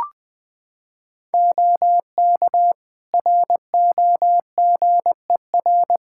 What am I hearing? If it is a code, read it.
Morse code, heard as a keyed tone, about 20 wpm, 700 Hz. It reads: OK ROGER
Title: Practice QSO - Exchange 4 - Ending - 20wpm